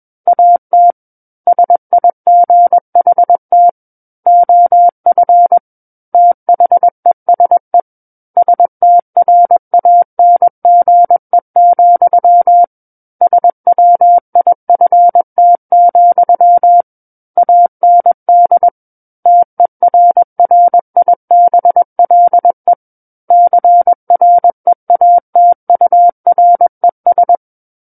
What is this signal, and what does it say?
Morse code, heard as a keyed tone, about 21 wpm, 700 Hz. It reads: AT SIGHT OF THESE STRANGE, SWIFT, AND TERRIBLE CREATURES
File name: Book One / 12-Chapter 12 - 21 WPM